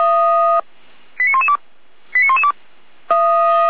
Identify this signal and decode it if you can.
Five-tone selective call (ZVEI), heard as quick runs of five tones, tones of about 70 ms. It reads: 89112 89112